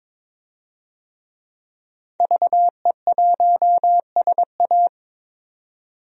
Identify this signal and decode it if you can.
Morse code, heard as a keyed tone, about 23 wpm, 700 Hz. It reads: VE1SA
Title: DXCC-100 Call Signs - 5 Characters - 22wpm